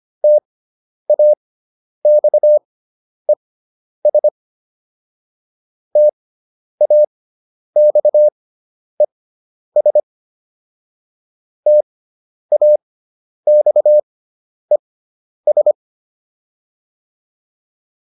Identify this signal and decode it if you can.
Morse code, heard as a keyed tone, about 25 wpm, spaced as for 10 wpm, 600 Hz. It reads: TAXES TAXES TAXES